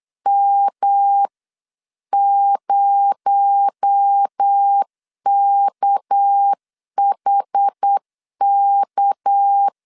Morse code, heard as a keyed tone, about 8 wpm, 800 Hz. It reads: M0KHK
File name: CW / DET / 1 9 1500C S C 8wpm 5653